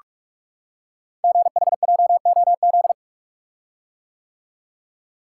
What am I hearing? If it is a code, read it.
Morse code, heard as a keyed tone, about 44 wpm, 700 Hz. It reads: GHJOZ
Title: Code Groups - 5 Letters - 45wpm